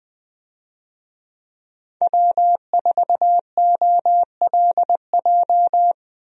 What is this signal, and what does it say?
Morse code, heard as a keyed tone, about 21 wpm, 700 Hz. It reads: W4OLJ
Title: USA Call Signs - 5 Characters - Repeated 2x - 20wpm